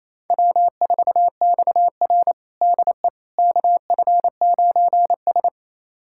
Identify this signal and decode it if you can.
Morse code, heard as a keyed tone, about 27 wpm, 700 Hz. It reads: W4XR DE KF9S
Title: USA Callsign DE Callsign - 4 Characters - 28wpm